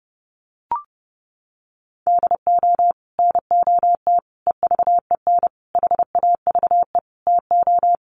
Morse code, heard as a keyed tone, about 30 wpm, 700 Hz. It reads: DO NOT EVEN HAVE TO
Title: Sets of 5 Words Encore2 30wpm